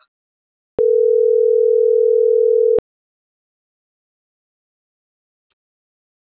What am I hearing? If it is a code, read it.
Ringback tone (North America), 440 + 480 Hz.